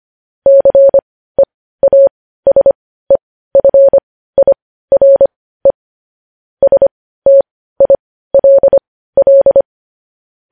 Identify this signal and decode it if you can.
Morse code, heard as a keyed tone, about 25 wpm, spaced as for 15 wpm, 550 Hz. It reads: CEASEFIRE STILL
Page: news headlines in morse code at 15 wpm